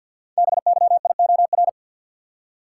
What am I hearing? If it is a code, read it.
Morse code, heard as a keyed tone, about 48 wpm, 700 Hz. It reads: BYIOR
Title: Code Groups - 5 Letters - 50wpm